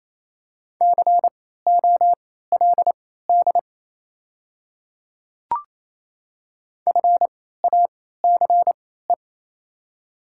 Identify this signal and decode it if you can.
Morse code, heard as a keyed tone, about 28 wpm, spaced as for 16 wpm, 690 Hz. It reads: COLD FACE
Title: 3x Character Spacing - Top 500 Words - 28wpm